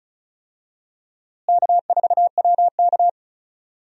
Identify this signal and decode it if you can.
Morse code, heard as a keyed tone, about 36 wpm, 700 Hz. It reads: K4WK
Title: USA Call Signs - 4 Characters - Repeated 2x - 35wpm